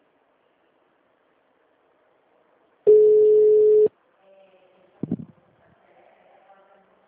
Ringback tone (Europe, 425 Hz).
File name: out- -2002- - - .2511